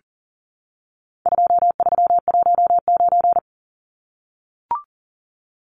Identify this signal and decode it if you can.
Morse code, heard as a keyed tone, about 40 wpm, 700 Hz. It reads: 2319
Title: Numbers - 4-Digit 40wpm